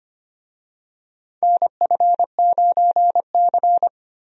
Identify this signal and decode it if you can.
Morse code, heard as a keyed tone, about 25 wpm, 700 Hz. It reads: NF9C
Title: USA Call Signs - 4 Characters - Repeated 3x - 25wpm